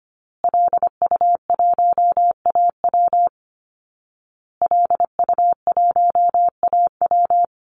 Morse code, heard as a keyed tone, about 25 wpm, 700 Hz. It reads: LU1AW LU1AW
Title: Call Signs Repeated 2x - 25wpm